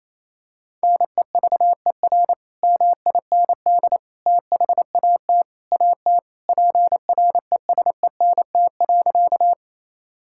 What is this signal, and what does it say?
Morse code, heard as a keyed tone, about 28 wpm, 700 Hz. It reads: NEVER MIND THAT AT PRESENT.
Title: Sentences from Top 1000 Words - Review Part 5 - 28wpm